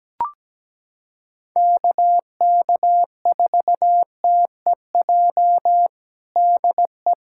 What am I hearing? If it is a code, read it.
Morse code, heard as a keyed tone, about 17 wpm, 700 Hz. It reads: KK4TEJ DE
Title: USA Callsign DE Callsign - 6 Characters - 17wpm